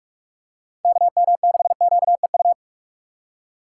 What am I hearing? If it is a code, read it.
Morse code, heard as a keyed tone, about 42 wpm, 700 Hz. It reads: KM6QEU
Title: USA Call Signs - 6 Characters - 4x Character Spacing - 45wpm